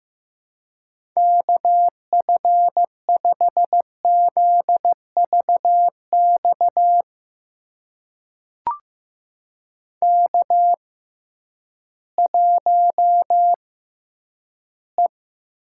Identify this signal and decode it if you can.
Morse code, heard as a keyed tone, about 15 wpm, spaced as for 10 wpm, 700 Hz. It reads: KF5ZVX K1E
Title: USA Call Signs - 6x Character Spacing - 15wpm